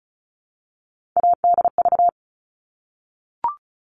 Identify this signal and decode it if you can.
Morse code, heard as a keyed tone, about 34 wpm, 700 Hz. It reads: ADV
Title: All Cut Numbers - 3-Digit 35wpm